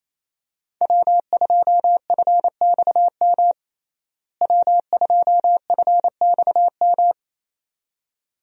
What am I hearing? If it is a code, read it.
Morse code, heard as a keyed tone, about 28 wpm, 700 Hz. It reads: W2FXM W2FXM
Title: USA Call Signs - 5 Characters - Repeated 2x - 28wpm